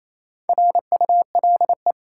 Morse code, heard as a keyed tone, about 28 wpm, 700 Hz. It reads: RULE
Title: Sentences from Top 400 Words - New Word Drill - 28wpm